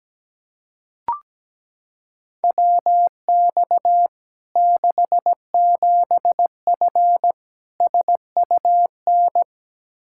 Morse code, heard as a keyed tone, about 17 wpm, 700 Hz. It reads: WX 67F SUN
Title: K1USN SST QSO Practice 17wpm